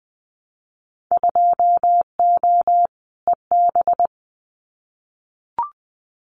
Morse code, heard as a keyed tone, about 20 wpm, 700 Hz. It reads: 2O EB
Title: Winter Field Day Exchanges Encore 20wpm